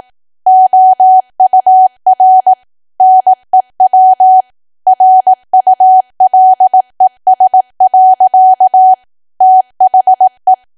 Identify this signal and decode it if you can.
Morse code, heard as a keyed tone, about 18 wpm, 750 Hz. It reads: OUR NEW RULES. THE
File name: cwa / W1AW-Level2 / 18WPM